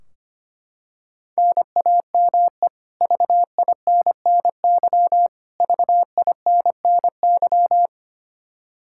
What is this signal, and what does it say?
Morse code, heard as a keyed tone, about 25 wpm, 700 Hz. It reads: NAME VINNY VINNY
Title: Practice QSO - Exchange 1 - Beginning - 25wpm